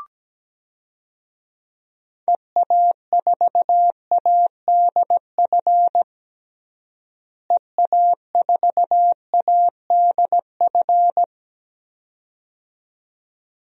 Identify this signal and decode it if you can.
Morse code, heard as a keyed tone, about 17 wpm, 700 Hz. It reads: EA4ADF EA4ADF
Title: DXCC-100 Call Signs - 6 Characters - Repeated 2x - 17wpm